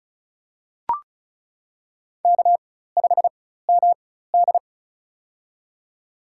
Morse code, heard as a keyed tone, about 36 wpm, spaced as for 16 wpm, 700 Hz. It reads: K5MD
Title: USA Call Signs - 4 Characters - 4x Character Spacing - 35wpm